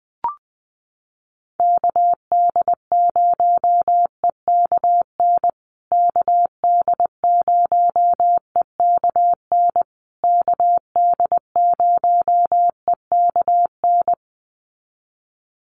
Morse code, heard as a keyed tone, about 20 wpm, 700 Hz. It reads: KD0EKN KD0EKN KD0EKN
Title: USA Call Signs Repeated 3x 20wpm